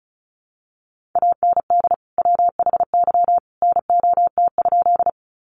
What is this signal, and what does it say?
Morse code, heard as a keyed tone, about 34 wpm, 700 Hz. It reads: AND WHY NOT?
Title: Sentences from Top 600 Words 35wpm